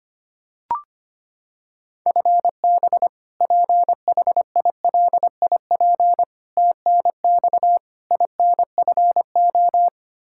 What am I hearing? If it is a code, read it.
Morse code, heard as a keyed tone, about 25 wpm, 700 Hz. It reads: FB PHILIP TNX INFO
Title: Practice QSO - Exchange 3 - More Info - 25wpm